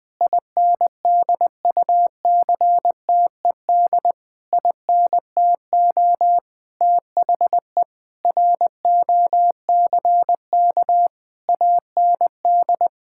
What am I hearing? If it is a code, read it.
Morse code, heard as a keyed tone, about 20 wpm, 700 Hz. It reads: INDUCTED INTO THE ROCK AND
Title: Sentences with 6-Word Phrases - 20wpm